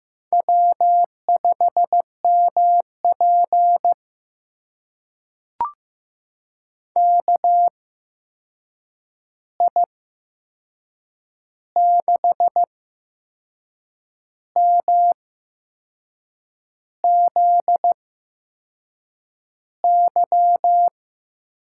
Morse code, heard as a keyed tone, about 15 wpm, spaced as for 6 wpm, 700 Hz. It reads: W5MP KI6MZY
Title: USA Call Signs - 8x Character Spacing - 15wpm